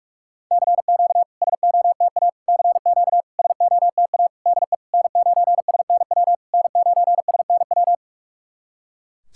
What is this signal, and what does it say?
Morse code, heard as a keyed tone, about 48 wpm, 690 Hz. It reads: CQ SOTA CQ SOTA DE N9SNW N9SNW
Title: SOTA QSO Practice 45wpm